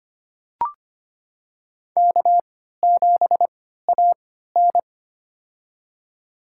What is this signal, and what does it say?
Morse code, heard as a keyed tone, about 25 wpm, spaced as for 14 wpm, 700 Hz. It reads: K7AN
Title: USA Call Signs - 4 Characters - 3x Character Spacing - 25wpm